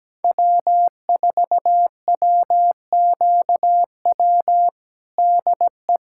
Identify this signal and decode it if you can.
Morse code, heard as a keyed tone, about 17 wpm, 700 Hz. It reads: W4WQW DE
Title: USA Callsign DE Callsign - 5 Characters - 17wpm